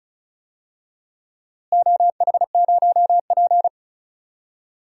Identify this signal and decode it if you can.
Morse code, heard as a keyed tone, about 34 wpm, 700 Hz. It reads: OH0P
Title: DXCC-100 Call Signs - 4 Characters - Repeated 2x - 35wpm